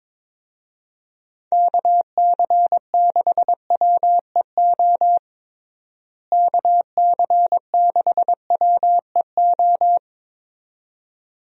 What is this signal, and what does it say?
Morse code, heard as a keyed tone, about 22 wpm, 700 Hz. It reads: KC6WEO KC6WEO
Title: USA Call Signs - 6 Characters - Repeated 2x - 22wpm